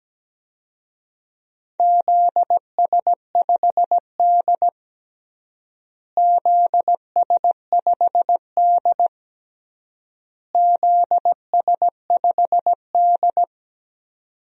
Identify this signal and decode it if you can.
Morse code, heard as a keyed tone, about 17 wpm, 700 Hz. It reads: ZS5D ZS5D ZS5D
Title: DXCC-100 Call Signs - 4 Characters - Repeated 3x - 17wpm